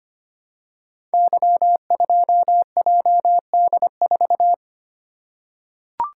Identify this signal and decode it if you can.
Morse code, heard as a keyed tone, about 25 wpm, 700 Hz. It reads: Y2JD4
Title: Code Groups - 5 Letter:Numbers - 25wpm